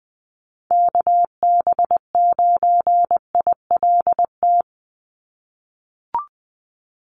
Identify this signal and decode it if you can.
Morse code, heard as a keyed tone, about 20 wpm, 700 Hz. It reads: KB9ILT
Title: NA SOTA Activator Callsigns 20wpm